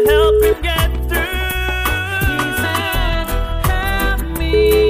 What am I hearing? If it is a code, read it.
Ringback tone (Europe, 425 Hz), with noise.